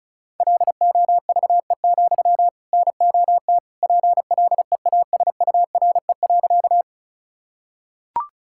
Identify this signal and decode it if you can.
Morse code, heard as a keyed tone, about 36 wpm, 700 Hz. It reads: LOVE, NOT PLEASURE.
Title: Sentences from Top 1600 Words 35wpm